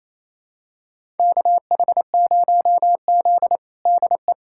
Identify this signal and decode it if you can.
Morse code, heard as a keyed tone, about 27 wpm, 690 Hz. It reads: KH0Z DE
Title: USA Callsign DE Callsign - 4 Characters - 28wpm